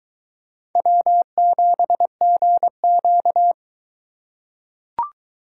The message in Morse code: W7GQ

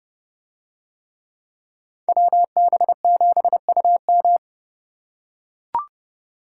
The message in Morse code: WB7UM

